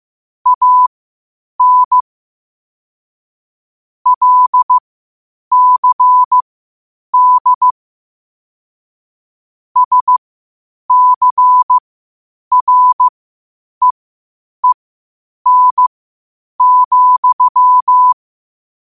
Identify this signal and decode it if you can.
Morse code, heard as a keyed tone, about 15 wpm, spaced as for 8 wpm, 1000 Hz. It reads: AN LCD SCREEN,
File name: qotdfiles / 8wpm